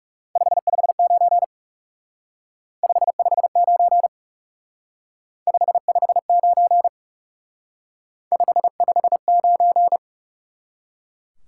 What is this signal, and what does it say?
Morse code, heard as a keyed tone, about 39 wpm, 700 Hz. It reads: 559 559 559 559